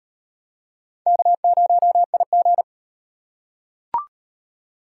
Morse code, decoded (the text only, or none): K0IG